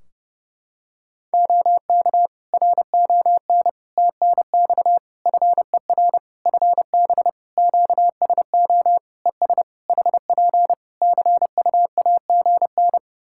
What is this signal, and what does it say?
Morse code, heard as a keyed tone, about 30 wpm, 700 Hz. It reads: OK RON TNX FER FB QSO ES HP CUAGN
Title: Practice QSO - Exchange 4 - Ending - 30wpm